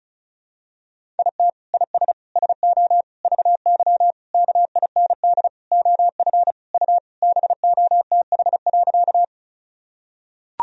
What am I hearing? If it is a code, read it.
Morse code, heard as a keyed tone, about 34 wpm, 700 Hz. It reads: IT IS SO VY KIND OF U BOTH.
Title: Sentences from Top 300 Words and Abbreviations 35wpm